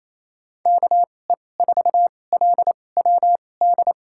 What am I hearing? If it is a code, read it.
Morse code, heard as a keyed tone, about 28 wpm, spaced as for 20 wpm, 700 Hz. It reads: KE4LWD
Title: USA Call Signs - 6 Characters - 2x Character Spacing - 28wpm